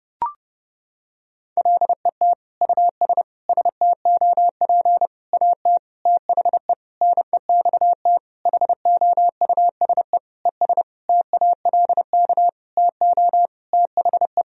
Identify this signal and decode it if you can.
Morse code, heard as a keyed tone, about 32 wpm, 700 Hz. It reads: LET US STOP AT THE NEXT HOUSE ES TALK TO THE
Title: Sentences from Top 400 Words and Abbreviations 30wpm